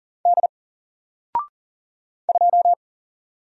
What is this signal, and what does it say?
Morse code, heard as a keyed tone, about 40 wpm, 700 Hz. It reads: D 2
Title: Single Letter-Number 40wpm